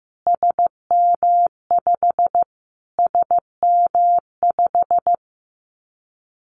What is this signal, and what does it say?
Morse code, heard as a keyed tone, about 15 wpm, 700 Hz. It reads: SM5 SM5
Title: Callsign Prefixes for Top 25 DXCC - Repeated 3x - 15wpm